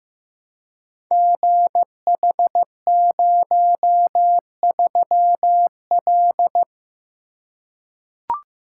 Morse code, heard as a keyed tone, about 15 wpm, 700 Hz. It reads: GH03L
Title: Code Groups - 5 Letter:Numbers - 15wpm